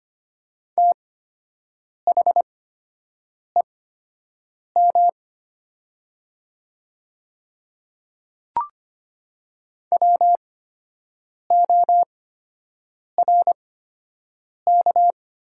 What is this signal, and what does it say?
Morse code, heard as a keyed tone, about 25 wpm, spaced as for 7 wpm, 700 Hz. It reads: THEM WORK
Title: 8x Character Spacing - Top 200 Words - 25wpm